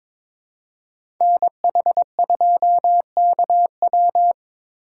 Morse code, heard as a keyed tone, about 22 wpm, 700 Hz. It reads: NH2KW
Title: USA Call Signs - 3x Character Spacing - 22wpm